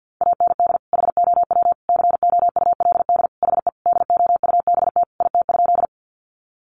Morse code, heard as a keyed tone, about 50 wpm, 700 Hz. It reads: AND HOW COULD HE DOUBT IT?
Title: Sentences from Top 1500 Words 50wpm